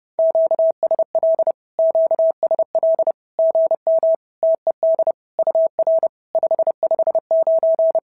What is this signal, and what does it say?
Morse code, heard as a keyed tone, about 30 wpm, 650 Hz. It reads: QSL QSL GM TED UR 559
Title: SOTA QSO Practice 30wpm